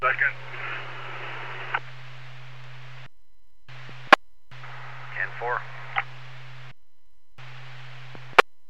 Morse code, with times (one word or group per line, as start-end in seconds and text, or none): none